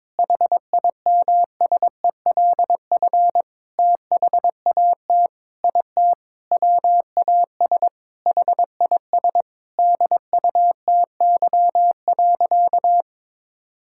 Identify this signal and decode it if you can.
Morse code, heard as a keyed tone, about 22 wpm, 700 Hz. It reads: HIMSELF THAT IT WAS HIS DUTY.
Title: Sentences from Top 1600 Words 22wpm